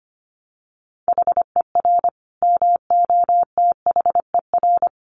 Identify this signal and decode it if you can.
Morse code, heard as a keyed tone, about 25 wpm, 700 Hz. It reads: HER MOTHER
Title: Sentences from Top 1200 Words 25wpm